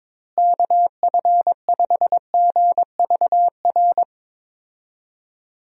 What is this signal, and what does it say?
Morse code, heard as a keyed tone, about 22 wpm, 700 Hz. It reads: KF5GVR
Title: USA Call Signs - 6 Characters - Repeated 3x - 22wpm